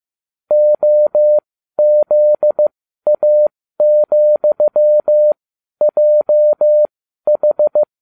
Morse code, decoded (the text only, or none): OZA,JH